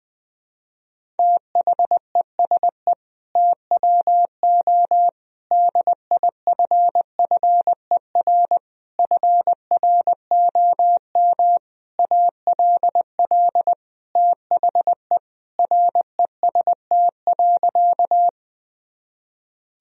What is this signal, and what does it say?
Morse code, heard as a keyed tone, about 20 wpm, 700 Hz. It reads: THESE TWO DIFFER FROM ALL THE REST.
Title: Sentences from Top 1000 Words - Review Part 1 - 20wpm